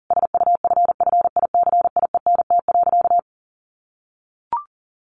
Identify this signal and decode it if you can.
Morse code, heard as a keyed tone, about 40 wpm, 700 Hz. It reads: SUFFICIENT.
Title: Sentences from Top 1400 Words 40wpm